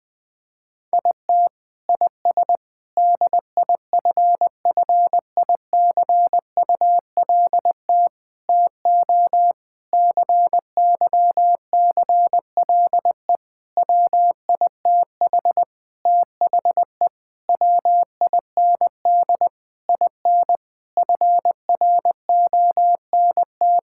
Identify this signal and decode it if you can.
Morse code, heard as a keyed tone, about 21 wpm, 700 Hz. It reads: IT IS DIFFICULT TO CYCLE WITH THE WIND IN FRONT
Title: Sentences from Top 1000 Words - Review Part 6 - 20wpm